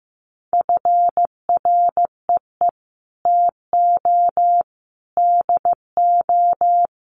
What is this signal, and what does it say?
Morse code, heard as a keyed tone, about 15 wpm, 700 Hz. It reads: FREE TO DO
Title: Sentences from Top 500 Words 15wpm